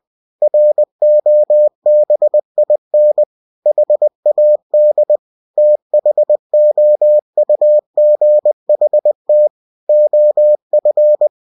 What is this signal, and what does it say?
Morse code, heard as a keyed tone, about 20 wpm, 600 Hz. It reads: ROBIN HAD THOUGHT OF
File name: cw-books / winnie-the-pooh / winnie-20wpm-page-0093